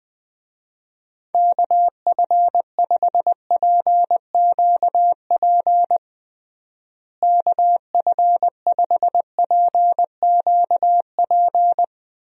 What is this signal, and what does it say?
Morse code, heard as a keyed tone, about 21 wpm, 700 Hz. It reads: KF5PQP KF5PQP
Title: USA Call Signs - 6 Characters - Repeated 2x - 20wpm